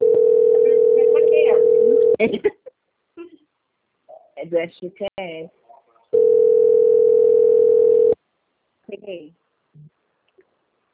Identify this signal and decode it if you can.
Ringback tone (North America), 440 + 480 Hz, with speech.
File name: out- - - - - .14460